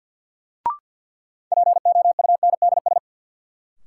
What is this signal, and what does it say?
Morse code, heard as a keyed tone, about 52 wpm, 700 Hz. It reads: POUNDS
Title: Sentences from Top 1000 Words - Review Part 2 - New Word Drill - 50wpm